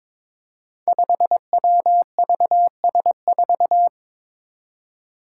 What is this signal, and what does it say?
Morse code, heard as a keyed tone, about 22 wpm, 700 Hz. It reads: 5WVS4
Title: Five Challenging Letter-Number Combinations 22wpm